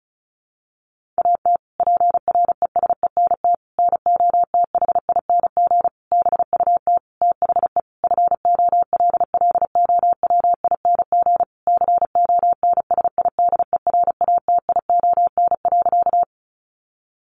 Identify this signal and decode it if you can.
Morse code, heard as a keyed tone, about 34 wpm, 700 Hz. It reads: AT PRESENT NOTHING BUT THE FOLLOWING CONSIDERATION.
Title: Sentences from Top 1500 Words 35wpm